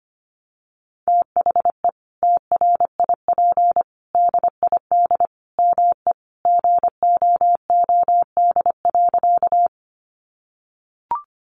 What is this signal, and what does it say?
Morse code, heard as a keyed tone, about 25 wpm, 700 Hz. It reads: THE TRIP DID ME GOOD.
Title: Sentences from Top 1300 Words 25wpm